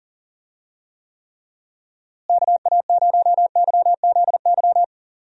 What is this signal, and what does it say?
Morse code, heard as a keyed tone, about 37 wpm, 700 Hz. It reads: KA0YZY
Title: USA Call Signs - 6 Characters - Repeated 3x - 40wpm